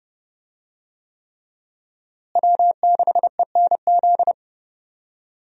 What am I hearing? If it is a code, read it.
Morse code, heard as a keyed tone, about 32 wpm, 700 Hz. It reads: W6ENZ
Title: USA Call Signs - 5 Characters - Repeated 3x - 30wpm